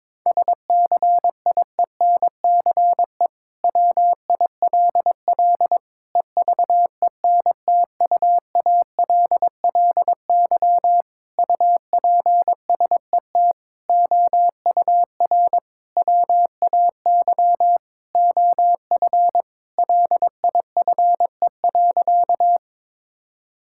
Morse code, heard as a keyed tone, about 22 wpm, 700 Hz. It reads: SCIENCE WILL EVENTUALLY UPSET OUR WAY OF LIFE.
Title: Sentences from Top 1000 Words - Review Part 6 - 22wpm